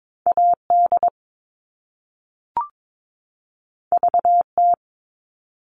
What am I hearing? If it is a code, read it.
Morse code, heard as a keyed tone, about 22 wpm, 700 Hz. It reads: AD VT
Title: All Cut Numbers - 2-Digit 22wpm